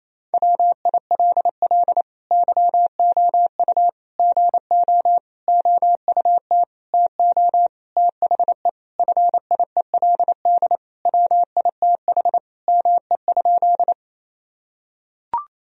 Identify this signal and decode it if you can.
Morse code, heard as a keyed tone, about 28 wpm, 700 Hz. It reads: WILL YOU GO OUT TO THE FIELD WITH ME?
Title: Sentences from Top 1000 Words - Review Part 2 - 28wpm